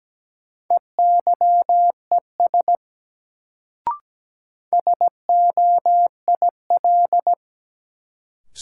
Morse code, heard as a keyed tone, about 17 wpm, 700 Hz. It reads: EYES SOIL
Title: Sentences from Top 1000 Words - Review Part 4 - New Word Drill - 17wpm